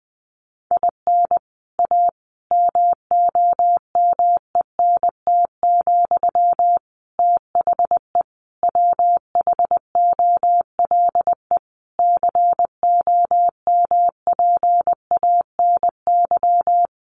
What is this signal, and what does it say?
Morse code, heard as a keyed tone, about 20 wpm, 700 Hz. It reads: IN A MOMENT, THE WHOLE COMPANY
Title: Sentences from Top 1000 Words 20wpm